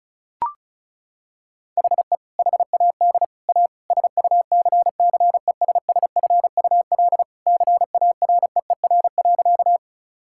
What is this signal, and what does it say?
Morse code, heard as a keyed tone, about 36 wpm, 690 Hz. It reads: HE HAD A SUCCESSFUL CAREER.
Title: Sentences from Top 1000 Words - Review Part 6 - 35wpm